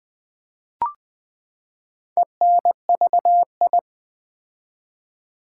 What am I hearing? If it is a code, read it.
Morse code, heard as a keyed tone, about 20 wpm, 700 Hz. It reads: ENVI
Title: Build A Word - 4 Letters At A Time - 11-Letter Words - 20wpm